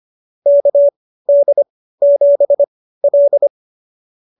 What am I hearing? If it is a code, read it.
Morse code, heard as a keyed tone, about 25 wpm, spaced as for 15 wpm, 590 Hz. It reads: KD7L
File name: qso222 15